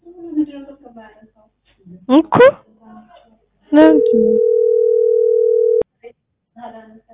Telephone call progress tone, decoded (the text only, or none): ringback tone